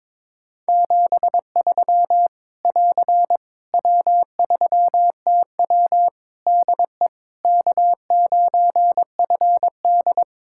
Morse code, heard as a keyed tone, about 21 wpm, 700 Hz. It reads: 73 <AR> W3TW DE K9FD
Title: Practice QSO - Exchange 4 - Ending - 22wpm